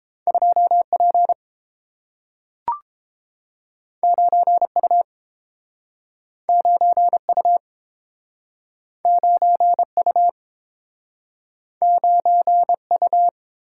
Morse code, heard as a keyed tone, about 32 wpm, 700 Hz. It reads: 2P 9U 9U 9U 9U